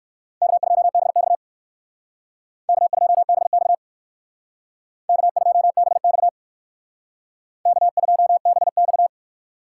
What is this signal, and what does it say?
Morse code, heard as a keyed tone, about 57 wpm, 700 Hz. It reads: K2BX K2BX K2BX K2BX